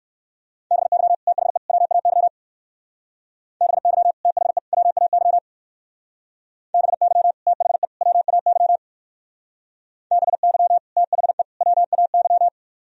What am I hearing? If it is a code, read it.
Morse code, heard as a keyed tone, about 55 wpm, 700 Hz. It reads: BY THE WAY BY THE WAY BY THE WAY BY THE WAY
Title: Speed-Racing - Sets of 3 Words 45wpm